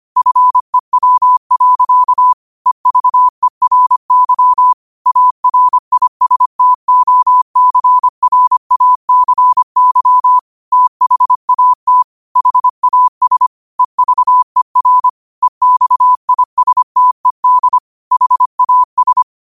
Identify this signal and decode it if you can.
Morse code, heard as a keyed tone, about 25 wpm, 1000 Hz. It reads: FEW. EVERY ARISTOCRACY THAT HAS EVER EXISTED HAS
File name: qotdfiles / 25wpm